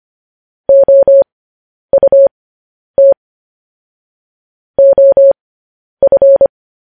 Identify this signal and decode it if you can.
Morse code, heard as a keyed tone, about 25 wpm, spaced as for 10 wpm, 550 Hz. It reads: OUT OF
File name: podcast / News headlines-1x-25x10